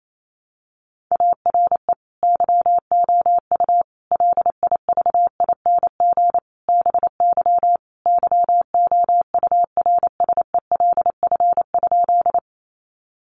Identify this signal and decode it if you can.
Morse code, heard as a keyed tone, about 28 wpm, 700 Hz. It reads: ARE YOU LIVING BY YOURSELF?
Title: Sentences from Top 1100 Words 28wpm